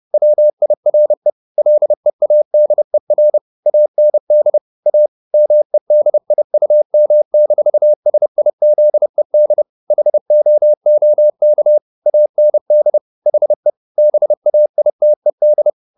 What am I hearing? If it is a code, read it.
Morse code, heard as a keyed tone, about 30 wpm, 600 Hz. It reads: WIRE LEADER AND A MEDIUM-SIZED HOOK AND HE BAITED
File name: the old man and the sea / the old man and the sea 1 30wpm